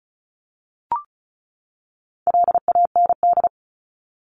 Morse code, heard as a keyed tone, about 34 wpm, 700 Hz. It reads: LAND
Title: Top 500 Words 35wpm